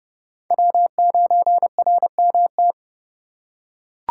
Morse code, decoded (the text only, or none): W9RMT